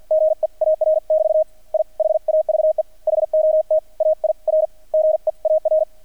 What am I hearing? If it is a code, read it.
Morse code, heard as a keyed tone, about 50 wpm, spaced as for 43 wpm, 630 Hz. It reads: GEAR <BT> I HAVE HOT AIR GEAR
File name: archived / CRSnet